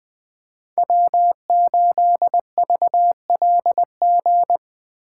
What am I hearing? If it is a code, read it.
Morse code, heard as a keyed tone, about 20 wpm, 700 Hz. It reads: W8VLG
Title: USA Call Signs - 5 Characters - 6x Character Spacing - 20wpm